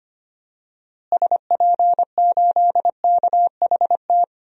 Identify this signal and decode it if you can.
Morse code, heard as a keyed tone, about 25 wpm, 700 Hz. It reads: SP8KHT
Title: DXCC-100 Call Signs - 6 Characters - Repeated 2x - 25wpm